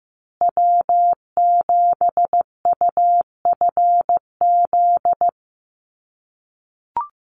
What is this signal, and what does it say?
Morse code, heard as a keyed tone, about 15 wpm, 700 Hz. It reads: W7UFZ
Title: USA Call Signs Repeated 3x 15wpm